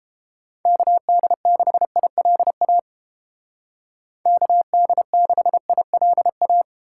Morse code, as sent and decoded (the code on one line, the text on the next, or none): -.- -.. -.... .. .-.. .- / -.- -.. -.... .. .-.. .-
KD6ILA KD6ILA